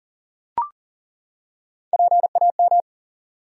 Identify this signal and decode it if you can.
Morse code, heard as a keyed tone, about 40 wpm, 700 Hz. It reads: PAM
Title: USA Common Names - Max 3 Letters - 40wpm